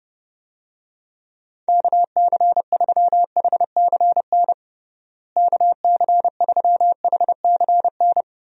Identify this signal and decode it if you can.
Morse code, heard as a keyed tone, about 32 wpm, 700 Hz. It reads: KC3HCN KC3HCN
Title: USA Call Signs - 6 Characters - Repeated 2x - 30wpm